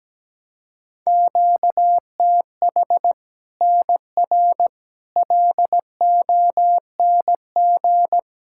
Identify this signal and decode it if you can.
Morse code, heard as a keyed tone, about 17 wpm, 700 Hz. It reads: QTH NR LONG
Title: K1USN SST QSO Practice 17wpm